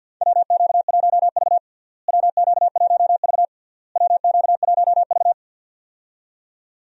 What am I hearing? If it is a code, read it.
Morse code, heard as a keyed tone, about 52 wpm, 700 Hz. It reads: WQ1V WQ1V WQ1V
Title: USA Call Signs - 4 Characters - Repeated 3x - 50wpm